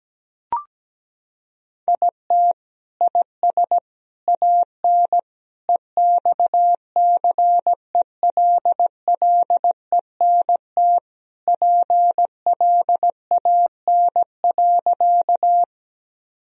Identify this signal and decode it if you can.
Morse code, heard as a keyed tone, about 17 wpm, 700 Hz. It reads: IT IS AN EXCELLENT PLAN.
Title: Sentences from Top 1000 Words - Review Part 2 - 17wpm